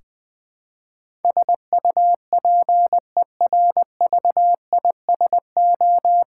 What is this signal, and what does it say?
Morse code, heard as a keyed tone, about 20 wpm, 700 Hz. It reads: SUPERVISO
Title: Practice QSO - Exchange 3 - More Info - 20wpm